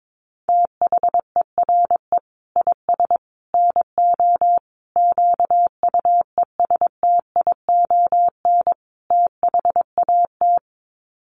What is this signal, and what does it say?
Morse code, heard as a keyed tone, about 22 wpm, 700 Hz. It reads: THERE IS NO QUESTION THAT
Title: Sets of 5 Words Encore 22wpm